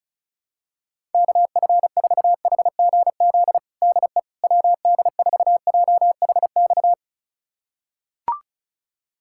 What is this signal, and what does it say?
Morse code, heard as a keyed tone, about 35 wpm, 700 Hz. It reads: KF4HGZ DE WD4JHX
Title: USA Callsign DE Callsign - 6 Characters - 35wpm